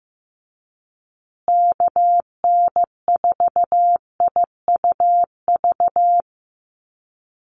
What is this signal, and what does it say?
Morse code, heard as a keyed tone, about 15 wpm, 700 Hz. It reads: KN4IUV
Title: USA Call Signs Repeated 3x 15wpm